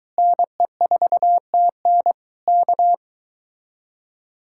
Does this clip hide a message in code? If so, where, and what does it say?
Yes: Morse code from 0.2 to 2.9 s, heard as a keyed tone, about 23 wpm, 700 Hz. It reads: NE4TN K